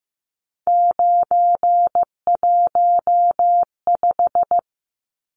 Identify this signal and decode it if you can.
Morse code, heard as a keyed tone, about 15 wpm, 700 Hz. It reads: 915
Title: Numbers - 3-Digit 15wpm